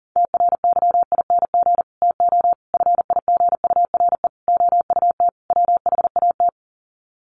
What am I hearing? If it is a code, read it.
Morse code, heard as a keyed tone, about 40 wpm, 700 Hz. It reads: TRYING TO FIGURE OUT WHAT
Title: Sets of 5 Words 40wpm